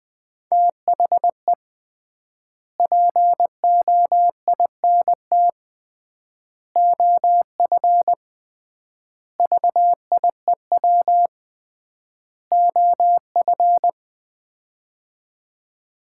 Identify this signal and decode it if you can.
Morse code, heard as a keyed tone, about 20 wpm, 700 Hz. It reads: THE POINT OF VIEW OF
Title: 3x Word-Spacing - Sets of 5 Words - 20wpm